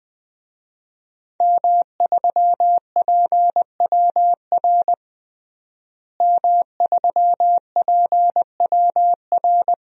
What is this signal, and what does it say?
Morse code, heard as a keyed tone, about 20 wpm, 700 Hz. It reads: M3PWR M3PWR
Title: DXCC-100 Call Signs - 5 Characters - Repeated 2x - 20wpm